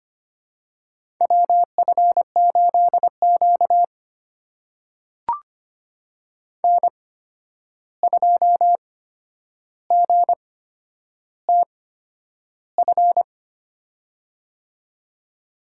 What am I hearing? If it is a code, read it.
Morse code, heard as a keyed tone, about 25 wpm, spaced as for 10 wpm, 700 Hz. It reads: WF8Q N2GTF
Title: USA Call Signs - 8x Character Spacing - 25wpm